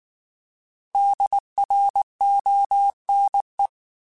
Morse code, heard as a keyed tone, about 19 wpm, 800 Hz. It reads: DRONE